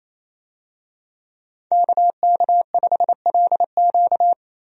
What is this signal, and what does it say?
Morse code, heard as a keyed tone, about 27 wpm, 700 Hz. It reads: KK5LQ
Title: USA Call Signs - 5 Characters - Repeated 2x - 28wpm